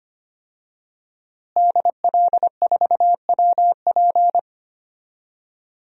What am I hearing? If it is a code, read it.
Morse code, heard as a keyed tone, about 25 wpm, 700 Hz. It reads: DL4WP
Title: DXCC-100 Call Signs - 5 Characters - Repeated 2x - 25wpm